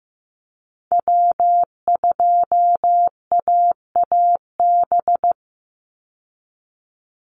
Morse code, heard as a keyed tone, about 15 wpm, 700 Hz. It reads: W2AAB
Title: Call Signs Repeated 2x - 15wpm